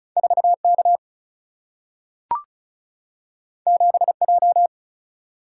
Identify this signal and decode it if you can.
Morse code, heard as a keyed tone, about 36 wpm, 700 Hz. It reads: 4K 7J
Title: Single-Letter Single-Number 35wpm